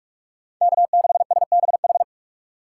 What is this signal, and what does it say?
Morse code, heard as a keyed tone, about 60 wpm, spaced as for 53 wpm, 700 Hz. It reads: K6SBH
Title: Speed Racing - USA Call Signs 45wpm